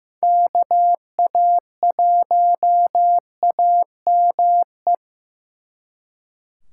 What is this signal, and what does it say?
Morse code, heard as a keyed tone, about 18 wpm, 700 Hz. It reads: KA1AME
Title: Speed Racing - USA 2x3 Call Signs 15wpm